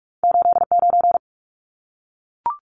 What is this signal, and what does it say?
Morse code, heard as a keyed tone, about 44 wpm, 700 Hz. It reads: 89
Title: Numbers - 2-Digit 45wpm